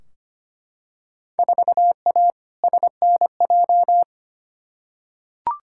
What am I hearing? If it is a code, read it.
Morse code, heard as a keyed tone, about 25 wpm, 700 Hz. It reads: 4A SNJ
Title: Field Day Exchanges Encore - 25wpm